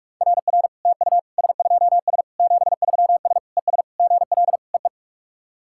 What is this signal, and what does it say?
Morse code, heard as a keyed tone, about 41 wpm, 690 Hz. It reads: RR TU S2S 73S ES GL EE
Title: SOTA QSO Practice 45wpm